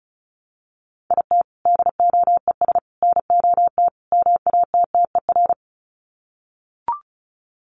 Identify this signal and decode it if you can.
Morse code, heard as a keyed tone, about 34 wpm, 700 Hz. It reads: IT DOES NOT MATTER
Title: Sets of 4 Words 35wpm